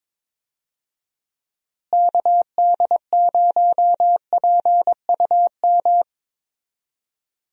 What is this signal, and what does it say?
Morse code, heard as a keyed tone, about 22 wpm, 700 Hz. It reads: KD0PUM
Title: USA Call Signs - 6 Characters - Repeated 3x - 22wpm